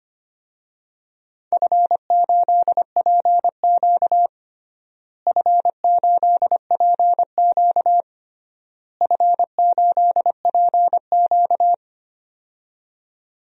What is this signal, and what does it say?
Morse code, heard as a keyed tone, about 25 wpm, 700 Hz. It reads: F8PQ F8PQ F8PQ
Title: DXCC-100 Call Signs - 4 Characters - Repeated 3x - 25wpm